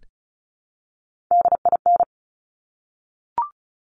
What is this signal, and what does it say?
Morse code, heard as a keyed tone, about 36 wpm, 700 Hz. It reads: DIN
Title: Top Three Letter Combinations in Words 35wpm